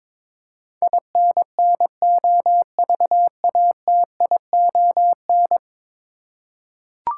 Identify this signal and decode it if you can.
Morse code, heard as a keyed tone, about 22 wpm, 700 Hz. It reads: INNOVATION
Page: morse-code-ninja-practice-22wpm